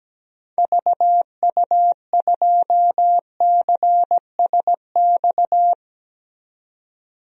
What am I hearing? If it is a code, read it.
Morse code, heard as a keyed tone, about 17 wpm, 700 Hz. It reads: VU2CSX
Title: DXCC-100 Call Signs - 6 Characters - 17wpm